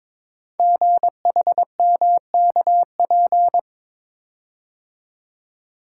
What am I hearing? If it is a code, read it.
Morse code, heard as a keyed tone, about 22 wpm, 700 Hz. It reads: GHMKP
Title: Code Groups - 5 Letters - 22wpm